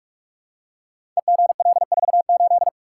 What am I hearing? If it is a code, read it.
Morse code, heard as a keyed tone, about 44 wpm, 700 Hz. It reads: EGL48
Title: Code Groups - 5 Letter:Numbers - 45wpm